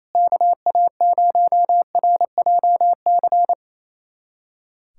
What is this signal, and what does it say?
Morse code, heard as a keyed tone, about 27 wpm, 700 Hz. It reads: KA0RJC
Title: USA Callsign DE Callsign - 6 Characters - 28wpm